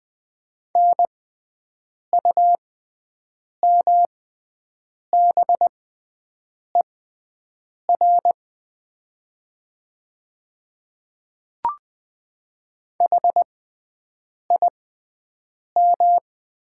Morse code, heard as a keyed tone, about 20 wpm, spaced as for 7 wpm, 700 Hz. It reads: NUMBER HIM
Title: 6x Character Spacing - Top 300 Words - 20wpm